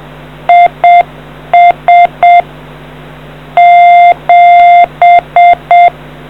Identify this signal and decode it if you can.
Morse code, heard as a keyed tone, about 7 wpm, 700 Hz. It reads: IS 7